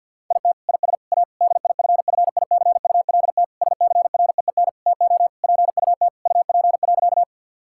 Morse code, heard as a keyed tone, about 50 wpm, 700 Hz. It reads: IT IS A DIFFICULT SCREEN TO PUT UP.
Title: Sentences from Top 1000 Words - Review Part 6 - 50wpm